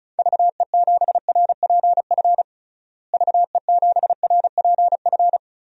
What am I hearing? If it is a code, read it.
Morse code, heard as a keyed tone, about 36 wpm, 700 Hz. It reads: VE7RPF VE7RPF
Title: DXCC-100 Call Signs - 6 Characters - Repeated 2x - 35wpm